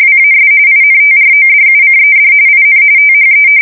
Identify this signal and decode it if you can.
RTTY, about 45.45 baud. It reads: WAS INTRODUCED IN THE
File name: fsk170 / usnavy1